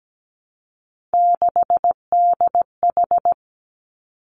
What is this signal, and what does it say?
Morse code, heard as a keyed tone, about 17 wpm, 700 Hz. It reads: 6DH
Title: Three Challenging Letter-Number Combinations 17wpm